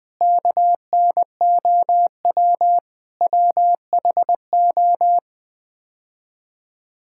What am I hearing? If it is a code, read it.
Morse code, heard as a keyed tone, about 20 wpm, 700 Hz. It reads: KNOW WHO
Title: Build A Phrase - 5-Word Phrases - 20wpm